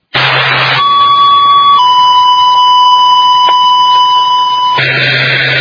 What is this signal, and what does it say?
Two-tone page, heard as a one-second tone, then a three-second tone, about 1130 then 990 Hz. It is heard from 0.8 to 4.8 s.